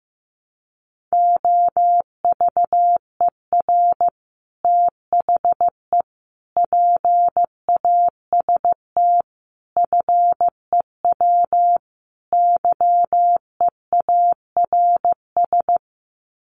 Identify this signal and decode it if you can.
Morse code, heard as a keyed tone, about 15 wpm, 700 Hz. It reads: OVER THE PAST FEW YEARS
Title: Sets of 5 Words Encore 15wpm